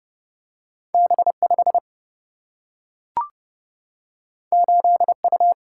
Morse code, heard as a keyed tone, about 30 wpm, 700 Hz. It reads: B 5 8 U
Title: Single-Letter Single-Number 30wpm